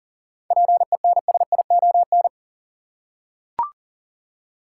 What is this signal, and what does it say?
Morse code, heard as a keyed tone, about 37 wpm, 700 Hz. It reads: PENSION
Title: Seven Letter Words - 40wpm